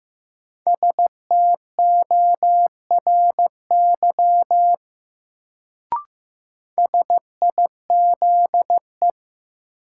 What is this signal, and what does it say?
Morse code, heard as a keyed tone, about 15 wpm, 700 Hz. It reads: STORY SIZE
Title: Sets of 3 Words - New Word Drill - 15wpm